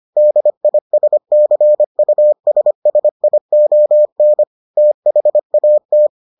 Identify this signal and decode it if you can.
Morse code, heard as a keyed tone, about 25 wpm, 600 Hz. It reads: DISCUSSION THAT